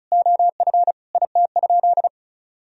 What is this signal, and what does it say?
Morse code, heard as a keyed tone, about 36 wpm, 700 Hz. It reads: OF IT?
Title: Sentences from Top 1000 Words - Review Part 1 - 35wpm